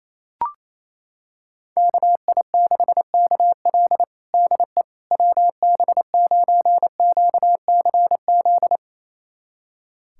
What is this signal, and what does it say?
Morse code, heard as a keyed tone, about 28 wpm, 700 Hz. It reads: KI6KL DE WB9QCZ
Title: USA Callsign DE Callsign 28wpm